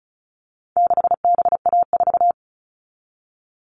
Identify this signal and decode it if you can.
Morse code, heard as a keyed tone, about 34 wpm, 700 Hz. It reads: 6BA4
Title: All Cut Numbers - 4-Digit 35wpm